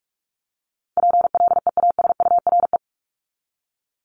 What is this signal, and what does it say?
Morse code, heard as a keyed tone, about 44 wpm, 700 Hz. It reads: PLEASURE
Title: Top 1501-1750 Words 45wpm